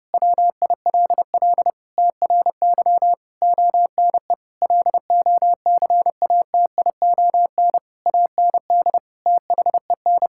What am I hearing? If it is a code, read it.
Morse code, heard as a keyed tone, about 30 wpm, 700 Hz. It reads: WILL TRY ONE LOCATION AND THEN
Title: Sentences from Top 1000 Words - Review Part 6 - 30wpm